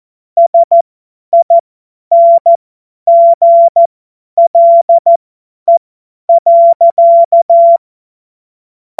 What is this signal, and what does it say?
Morse code, heard as a keyed tone, about 14 wpm, spaced as for 10 wpm, 680 Hz. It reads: SINGLE.